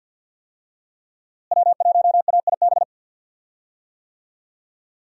Morse code, heard as a keyed tone, about 49 wpm, 700 Hz. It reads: W1AIB
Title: USA Call Signs - 5 Characters - Repeated 2x - 50wpm